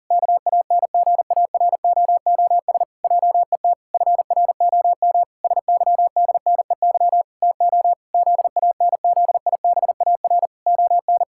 Morse code, heard as a keyed tone, about 40 wpm, 700 Hz. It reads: KANGAROOS JET FROM SYDNEY TO ZANZIBAR ON
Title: Pangram Challenge v2 - 40wpm